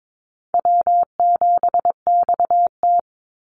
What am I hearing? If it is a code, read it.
Morse code, heard as a keyed tone, about 22 wpm, 700 Hz. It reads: W7XT
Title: Easy Call Signs Repeated 3x - 22wpm